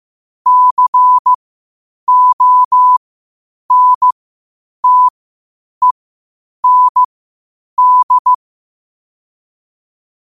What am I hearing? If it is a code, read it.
Morse code, heard as a keyed tone, about 15 wpm, spaced as for 8 wpm, 1000 Hz. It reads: CONTEND
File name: qotdfiles / 8wpm